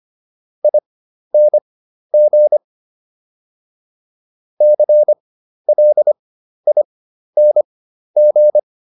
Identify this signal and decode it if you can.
Morse code, heard as a keyed tone, about 25 wpm, spaced as for 12 wpm, 600 Hz. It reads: ING CLING